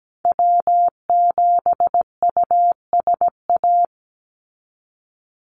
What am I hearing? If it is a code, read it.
Morse code, heard as a keyed tone, about 17 wpm, 700 Hz. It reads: W7USA
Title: NA SOTA Activator Callsigns 17wpm